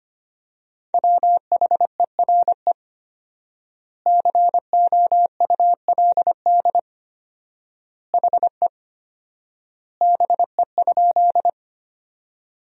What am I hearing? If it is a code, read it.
Morse code, heard as a keyed tone, about 25 wpm, 700 Hz. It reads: WHERE COULD HE BE?
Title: 4x Word-Spacing - Sentences from Top 200 Words - 25wpm